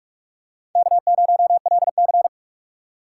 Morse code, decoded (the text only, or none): K0LC